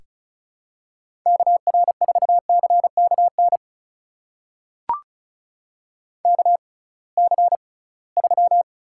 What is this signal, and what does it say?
Morse code, heard as a keyed tone, about 36 wpm, spaced as for 23 wpm, 690 Hz. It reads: KR4CKN KC3
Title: USA Call Signs - 6 Characters - 6x Character Spacing - 35wpm